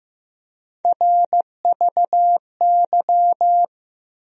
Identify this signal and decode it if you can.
Morse code, heard as a keyed tone, about 15 wpm, 700 Hz. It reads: RVY
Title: USA Call Sign Postfixes - 3 Letters 15wpm